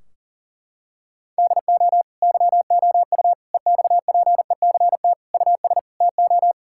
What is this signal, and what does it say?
Morse code, heard as a keyed tone, about 40 wpm, 700 Hz. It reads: DO YOU EXPECT US TO
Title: Sentences from Top 1600 Words 40wpm